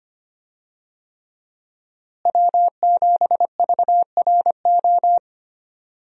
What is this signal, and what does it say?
Morse code, heard as a keyed tone, about 25 wpm, 700 Hz. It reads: W7VRO
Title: USA Call Signs - 5 Characters - Repeated 3x - 25wpm